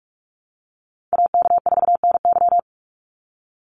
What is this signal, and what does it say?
Morse code, heard as a keyed tone, about 44 wpm, 700 Hz. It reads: AK4NY
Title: Call Signs Repeated 2x - 45wpm